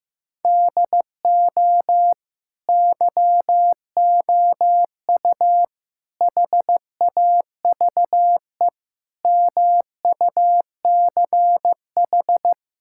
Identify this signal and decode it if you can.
Morse code, heard as a keyed tone, about 15 wpm, 700 Hz. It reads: DO YOU HAVE MUCH
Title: Sentences from Top 1000 Words - Review Part 6 - 15wpm